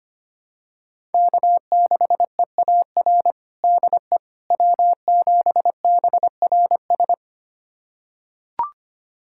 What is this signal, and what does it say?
Morse code, heard as a keyed tone, about 25 wpm, 700 Hz. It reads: K6EAR DE W7BRS
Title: USA Callsign DE Callsign - 5 Characters - 25wpm